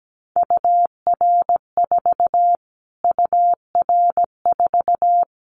Morse code, heard as a keyed tone, about 17 wpm, 700 Hz. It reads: UR4 UR4
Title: Callsign Prefixes for Top 25 DXCC - Repeated 3x - 17wpm